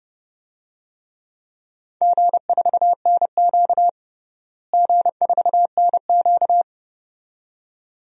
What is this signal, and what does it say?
Morse code, heard as a keyed tone, about 30 wpm, 700 Hz. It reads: G4NQ G4NQ
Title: DXCC-100 Call Signs - 4 Characters - Repeated 2x - 30wpm